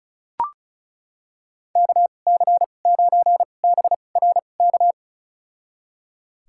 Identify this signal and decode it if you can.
Morse code, heard as a keyed tone, about 36 wpm, spaced as for 25 wpm, 700 Hz. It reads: KC9BRK
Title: USA Call Signs - 6 Characters - 2x Character Spacing - 35wpm